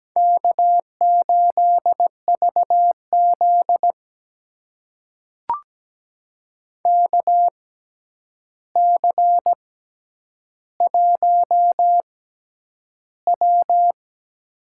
Morse code, heard as a keyed tone, about 17 wpm, spaced as for 9 wpm, 700 Hz. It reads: K8VZ KC1W
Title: USA Call Signs - 4 Characters - 6x Character Spacing - 17wpm